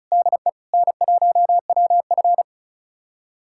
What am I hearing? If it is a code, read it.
Morse code, heard as a keyed tone, about 36 wpm, 700 Hz. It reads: DE N1WF
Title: USA Callsign DE Callsign - 4 Characters - 35wpm